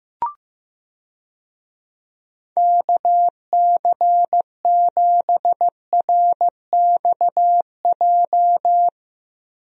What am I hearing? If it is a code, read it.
Morse code, heard as a keyed tone, about 15 wpm, 700 Hz. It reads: KC7RXJ
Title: USA Call Signs - 6 Characters - Repeated 3x - 15wpm